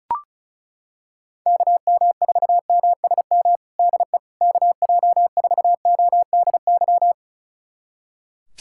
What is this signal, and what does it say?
Morse code, heard as a keyed tone, about 36 wpm, 700 Hz. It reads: KM4MSM DE KJ4ODY
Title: USA Callsign DE Callsign 35wpm